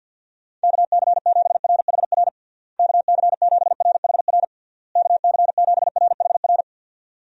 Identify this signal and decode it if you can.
Morse code, heard as a keyed tone, about 50 wpm, 700 Hz. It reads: KC7RHR KC7RHR KC7RHR
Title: USA Call Signs - 6 Characters - Repeated 3x - 50wpm